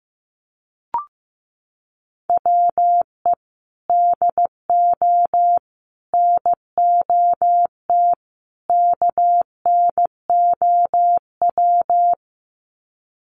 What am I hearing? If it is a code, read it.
Morse code, heard as a keyed tone, about 15 wpm, 700 Hz. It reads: WE DO NOT KNOW
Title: Sets of 4 Words 15wpm